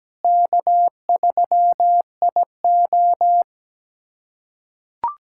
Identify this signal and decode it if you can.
Morse code, heard as a keyed tone, about 17 wpm, 700 Hz. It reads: K3IO
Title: USA Callsign DE Callsign - 4 Characters - 17wpm